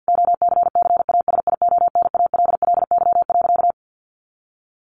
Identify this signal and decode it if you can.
Morse code, heard as a keyed tone, about 50 wpm, 700 Hz. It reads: OCCASIONALLY.
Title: Sentences from Top 1300 Words 50wpm